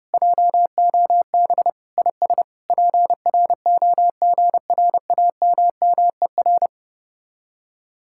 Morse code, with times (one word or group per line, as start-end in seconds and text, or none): 0.1-1.7 JOB
2.0-2.4 IS
2.7-6.7 PROGRAMMER